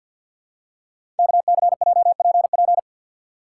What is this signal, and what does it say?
Morse code, heard as a keyed tone, about 48 wpm, 690 Hz. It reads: KCJPP
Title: Code Groups - 5 Letters - 50wpm